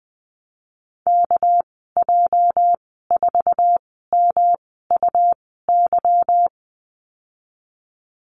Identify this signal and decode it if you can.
Morse code, heard as a keyed tone, about 20 wpm, spaced as for 15 wpm, 700 Hz. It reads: KJ4MUY
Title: 213 Lesson 34g - Number 8 - Call Signs 20wpm